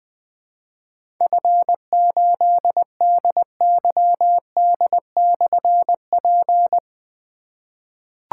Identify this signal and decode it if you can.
Morse code, heard as a keyed tone, about 20 wpm, 700 Hz. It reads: F8DYD/P
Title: Difficult Call Signs 20wpm